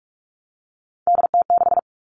Morse code, heard as a keyed tone, about 44 wpm, 700 Hz. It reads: DT6
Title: All Cut Numbers - 3-Digit 45wpm